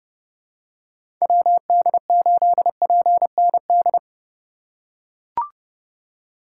Morse code, heard as a keyed tone, about 32 wpm, 700 Hz. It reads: WD8PND
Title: USA Call Signs - 6 Characters - 4x Character Spacing - 30wpm